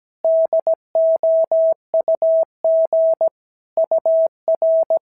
Morse code, heard as a keyed tone, about 17 wpm, 650 Hz. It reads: DOUG UR
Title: SOTA QSO Practice 17wpm